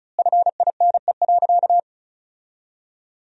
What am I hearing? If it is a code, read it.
Morse code, heard as a keyed tone, about 34 wpm, 700 Hz. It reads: FINE.
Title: 2x Word-Spacing - Sentences from Top 500 Words - 35wpm